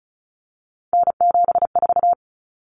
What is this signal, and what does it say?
Morse code, heard as a keyed tone, about 34 wpm, 700 Hz. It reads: N74
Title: Common Cut Numbers - 3-Digit 35wpm